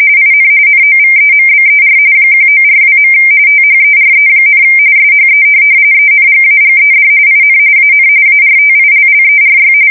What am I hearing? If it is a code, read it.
RTTY, about 45.45 baud. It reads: IN NOVEMBER 1942. THERE, SHE ENGAGED IN AN ARTILLERY
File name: fsk170 / USS-Massachusetts